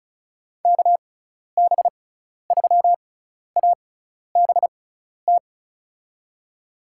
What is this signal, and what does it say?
Morse code, heard as a keyed tone, about 36 wpm, spaced as for 12 wpm, 700 Hz. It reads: KB3ABT
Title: USA Call Signs - 6 Characters - 6x Character Spacing - 35wpm